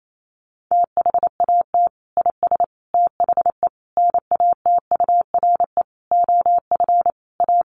Morse code, heard as a keyed tone, about 28 wpm, 700 Hz. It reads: THAT IS THE NATURE OF A
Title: Sentences from Top 1400 Words 28wpm